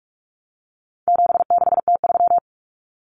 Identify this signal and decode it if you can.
Morse code, heard as a keyed tone, about 44 wpm, 700 Hz. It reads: 76T3
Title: Common Cut Numbers - 4-Digit 45wpm